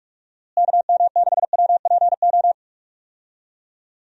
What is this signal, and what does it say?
Morse code, heard as a keyed tone, about 46 wpm, 700 Hz. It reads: KM6WPO
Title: USA Call Signs - 6 Characters - Repeated 2x - 45wpm